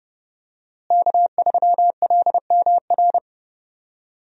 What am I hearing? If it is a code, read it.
Morse code, heard as a keyed tone, about 30 wpm, 700 Hz. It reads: K3LMR
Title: USA Call Signs - 5 Characters - Repeated 2x - 30wpm